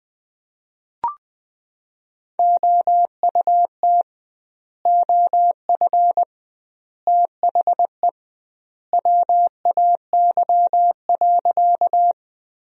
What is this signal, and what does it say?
Morse code, heard as a keyed tone, about 20 wpm, 700 Hz. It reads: OUT OF THE WAY.
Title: 2x Word-Spacing - Sentences from Top 100 Words - 20wpm